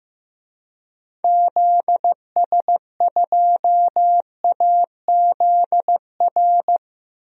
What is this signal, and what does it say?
Morse code, heard as a keyed tone, about 16 wpm, 700 Hz. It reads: ZS2AZR
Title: DXCC-100 Call Signs - 6 Characters - 15wpm